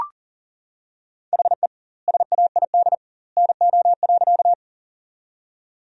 Morse code, heard as a keyed tone, about 40 wpm, 700 Hz. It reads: HE SAID NO.
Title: 2x Word-Spacing - Sentences from Top 100 Words - 40wpm